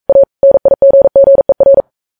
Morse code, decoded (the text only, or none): A NIGGER